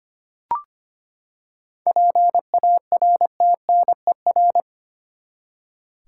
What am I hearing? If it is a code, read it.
Morse code, heard as a keyed tone, about 25 wpm, 700 Hz. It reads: PARTNER
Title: Seven Letter Words - 25wpm